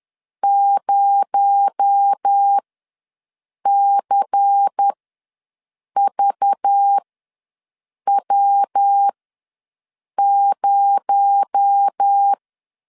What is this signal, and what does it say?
Morse code, heard as a keyed tone, about 10 wpm, spaced as for 6 wpm, 800 Hz. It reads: 0CVW0
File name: LEZIONI / 6 PKX0 / 6 PKX0 ws 10wpm 4220